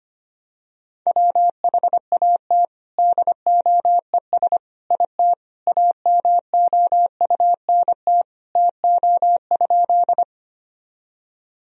Morse code, heard as a keyed tone, about 25 wpm, 700 Hz. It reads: WHAT DOES IT AMOUNT TO?
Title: Sentences from Top 1000 Words - Review Part 4 - 25wpm